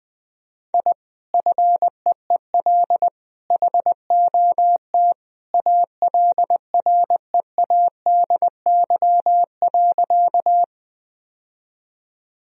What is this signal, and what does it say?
Morse code, heard as a keyed tone, about 20 wpm, 700 Hz. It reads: I FEEL HOT ALREADY.
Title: Sentences from Top 1000 Words - Review Part 3 - 20wpm